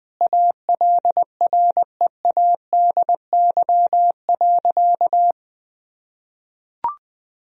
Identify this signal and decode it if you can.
Morse code, heard as a keyed tone, about 20 wpm, 700 Hz. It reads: ALREADY.
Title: Sentences from Top 1000 Words - Review Part 2 - 20wpm